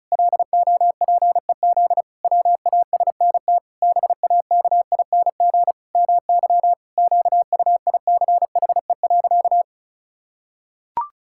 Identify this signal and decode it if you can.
Morse code, heard as a keyed tone, about 34 wpm, 700 Hz. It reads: LOPEZ WASNT BAKING MY QUICHE.
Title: Pangram Challenge v2 - 35wpm